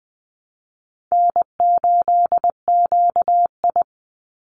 Morse code, heard as a keyed tone, about 20 wpm, 700 Hz. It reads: N8QI
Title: USA Call Signs 20wpm